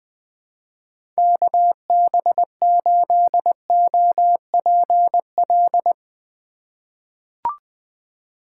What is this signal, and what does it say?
Morse code, heard as a keyed tone, about 20 wpm, 700 Hz. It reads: KB8OPL
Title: USA Call Signs - 6 Characters - 2x Character Spacing - 20wpm